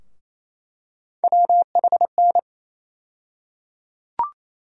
Morse code, heard as a keyed tone, about 28 wpm, 700 Hz. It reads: WHN
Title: USA Call Sign Postfixes - 3 Letters 28wpm